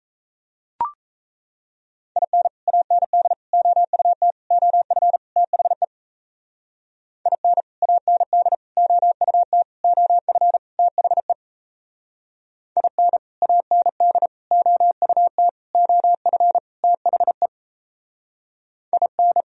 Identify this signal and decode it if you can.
Morse code, heard as a keyed tone, about 35 wpm, 700 Hz. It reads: IN AND OUT OF THE IN AND OUT OF THE IN AND OUT OF THE IN
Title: Speed-Racing - Sets of 5 Words 28wpm